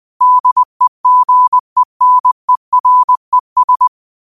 Morse code, heard as a keyed tone, about 20 wpm, 1000 Hz. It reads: DEGENERES